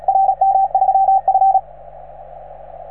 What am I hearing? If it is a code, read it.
Morse code, heard as a keyed tone, about 38 wpm, 770 Hz, with radio noise. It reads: RM3F